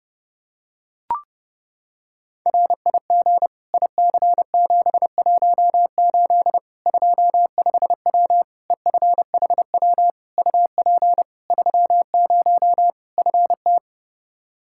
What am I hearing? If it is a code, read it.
Morse code, heard as a keyed tone, about 32 wpm, 700 Hz. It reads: RIG IC718 25W EFHW UP 30 FT
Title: Practice QSO - Exchange 2 - Info - 30wpm